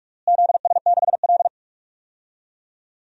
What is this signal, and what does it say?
Morse code, heard as a keyed tone, about 46 wpm, 700 Hz. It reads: ZS6L
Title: DXCC-100 Call Signs - 4 Characters - Repeated 2x - 45wpm